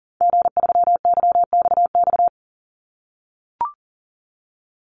Morse code, heard as a keyed tone, about 40 wpm, 700 Hz. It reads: G3YXX
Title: Call Signs - 40wpm